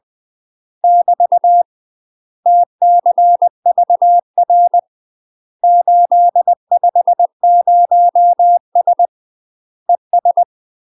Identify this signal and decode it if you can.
Morse code, heard as a keyed tone, about 20 wpm, 700 Hz. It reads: <BT> TCVR 850S ES
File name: UK MASTER QSOs TRAINING / 20WPM / UK MASTER QSOs 0029